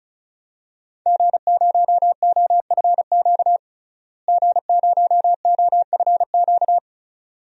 Morse code, heard as a keyed tone, about 36 wpm, 700 Hz. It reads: G0OFQ G0OFQ
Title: DXCC-100 Call Signs - 5 Characters - Repeated 2x - 35wpm